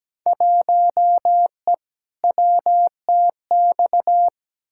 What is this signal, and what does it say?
Morse code, heard as a keyed tone, about 17 wpm, 700 Hz. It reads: 1E WTX
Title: Field Day Exchanges - 17wpm